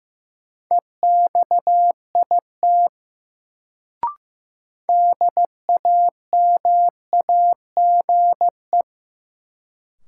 Morse code, heard as a keyed tone, about 15 wpm, 700 Hz. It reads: EXIT DAMAGE
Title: Sentences from Top 1100 Words - New Word Drill - 15wpm